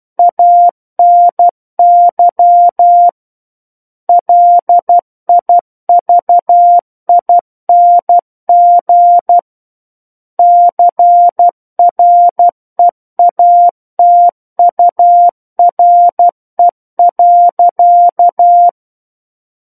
Morse code, heard as a keyed tone, about 12 wpm, 700 Hz. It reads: ANY LIVING CREATURE.
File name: Book One / 03-Chapter 3 - 12 WPM